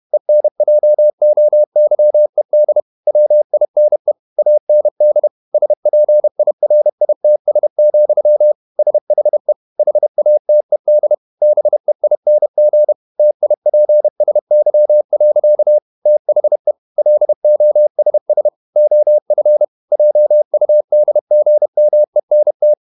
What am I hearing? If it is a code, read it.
Morse code, heard as a keyed tone, about 31 wpm, 600 Hz. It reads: ENJOYED WINE AND SPIRITS, SHE HATED BEING TIPSY. THE LOSS OF JUDGMENT
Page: Never 31wpm | CW med Gnister